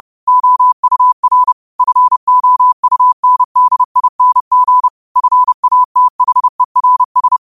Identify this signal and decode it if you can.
Morse code, heard as a keyed tone, about 30 wpm, 1000 Hz. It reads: OUR FOUNDING FATHERS
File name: qotdfiles / 30wpm